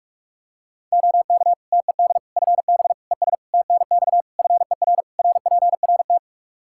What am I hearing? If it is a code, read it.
Morse code, heard as a keyed tone, about 46 wpm, 700 Hz. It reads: OK TED FB ES TNX FER RPRT
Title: Practice QSO - Exchange 2 - Info - 45wpm